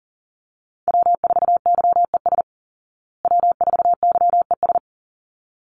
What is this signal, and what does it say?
Morse code, heard as a keyed tone, about 40 wpm, 700 Hz. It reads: W4YES W4YES
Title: Call Signs Repeated 3x - 40wpm